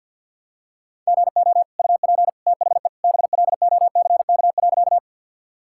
Morse code, heard as a keyed tone, about 50 wpm, 700 Hz. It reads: GO UP THE BLOCK.
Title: Sentences from Top 1600 Words 50wpm